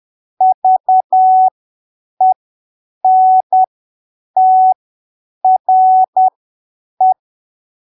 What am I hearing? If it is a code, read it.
Morse code, heard as a keyed tone, about 10 wpm, spaced as for 7 wpm, 750 Hz. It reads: VENTRE
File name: arquivos / texto / texto 27 10ppm